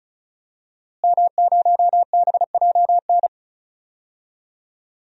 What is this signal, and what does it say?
Morse code, heard as a keyed tone, about 36 wpm, 700 Hz. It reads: M0BJN
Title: DXCC-100 Call Signs - 5 Characters - Repeated 3x - 35wpm